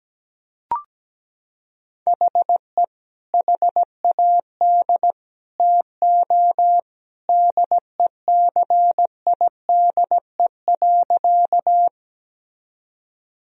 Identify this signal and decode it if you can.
Morse code, heard as a keyed tone, about 17 wpm, 700 Hz. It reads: HE HAD TO DECIDE.
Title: Sentences from Top 1000 Words - Review Part 3 - 17wpm